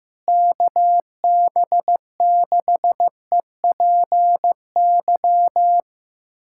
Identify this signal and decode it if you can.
Morse code, heard as a keyed tone, about 15 wpm, 700 Hz. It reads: KB6EPY
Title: USA Callsign DE Callsign 15wpm